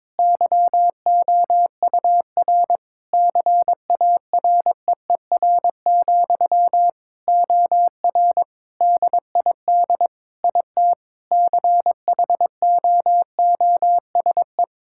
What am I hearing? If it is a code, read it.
Morse code, heard as a keyed tone, about 22 wpm, 700 Hz. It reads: YOUR CAREER, OR DID IT CHOOSE
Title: Sentences from Top 1000 Words - Review Part 6 - 22wpm